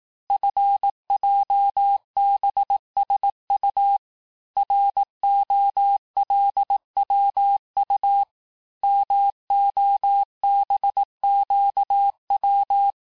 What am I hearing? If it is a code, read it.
Morse code, heard as a keyed tone, about 18 wpm, 800 Hz. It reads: FJBSU ROLWU MOBQW
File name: cours-du-lundi files / cours-1594--teste-chiffre-vitesse-17mpm